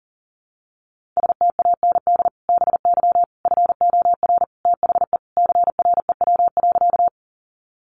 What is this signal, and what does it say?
Morse code, heard as a keyed tone, about 40 wpm, 700 Hz. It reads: STAND BY FOR THE CREW.
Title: Sentences from Top 1500 Words 40wpm